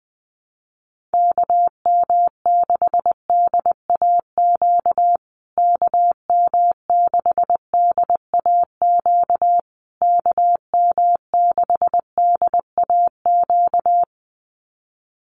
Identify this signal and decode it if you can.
Morse code, heard as a keyed tone, about 20 wpm, 700 Hz. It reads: KM6DAQ KM6DAQ KM6DAQ
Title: USA Call Signs Repeated 3x 20wpm